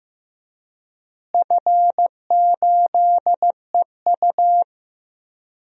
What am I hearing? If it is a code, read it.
Morse code, heard as a keyed tone, about 15 wpm, 700 Hz. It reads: F8EU
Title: DXCC-100 Call Signs - 4 Characters - Repeated 2x - 15wpm